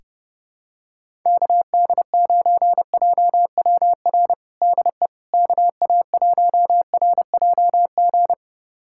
Morse code, heard as a keyed tone, about 30 wpm, 700 Hz. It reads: KD9JWR DE KA1RJG
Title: USA Callsign DE Callsign 30wpm